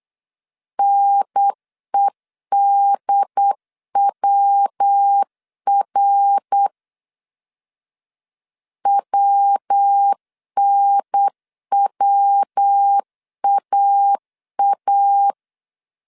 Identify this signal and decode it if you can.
Morse code, heard as a keyed tone, about 8 wpm, 800 Hz. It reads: NEDWR WNWAA
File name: CW / DET / 1 3 1500C S B 8wpm 4141